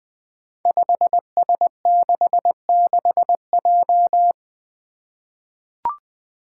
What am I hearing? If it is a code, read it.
Morse code, heard as a keyed tone, about 20 wpm, 700 Hz. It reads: 5S66J
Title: Five Challenging Letter-Number Combinations 20wpm